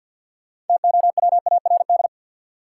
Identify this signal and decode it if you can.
Morse code, heard as a keyed tone, about 53 wpm, 700 Hz. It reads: TOWARD
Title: Sets of 5 Words - New Word Drill - 50wpm